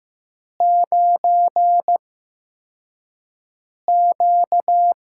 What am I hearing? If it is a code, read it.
Morse code, heard as a keyed tone, about 15 wpm, spaced as for 4 wpm, 700 Hz. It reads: 9Q
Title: USA Call Signs - 4 Characters - 8x Character Spacing - 15wpm